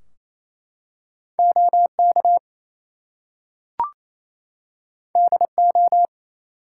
Morse code, heard as a keyed tone, about 27 wpm, 700 Hz. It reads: OK DO
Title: Two Letter Words - 28wpm